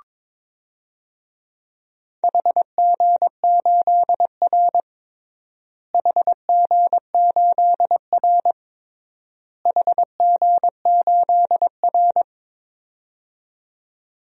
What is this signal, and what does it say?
Morse code, heard as a keyed tone, about 22 wpm, 700 Hz. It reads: HG8R HG8R HG8R
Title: DXCC-100 Call Signs - 4 Characters - Repeated 3x - 22wpm